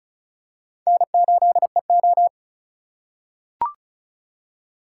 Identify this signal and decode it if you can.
Morse code, heard as a keyed tone, about 36 wpm, 700 Hz. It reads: N8EO
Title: USA Call Signs - 4 Characters - 3x Character Spacing - 35wpm